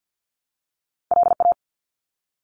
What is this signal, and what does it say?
Morse code, heard as a keyed tone, about 50 wpm, 700 Hz. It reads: LA
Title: ARRL Sections 50wpm